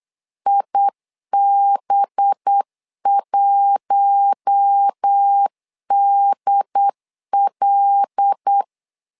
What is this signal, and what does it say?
Morse code, heard as a keyed tone, about 8 wpm, 800 Hz. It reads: IB1DL
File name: LEZIONI / 5 LBVC / 5 LB 8wpm 2816